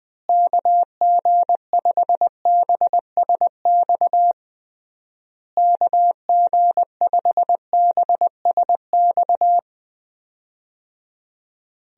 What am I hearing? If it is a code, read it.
Morse code, heard as a keyed tone, about 21 wpm, 700 Hz. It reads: KG5BSX KG5BSX
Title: USA Call Signs - 6 Characters - Repeated 3x - 20wpm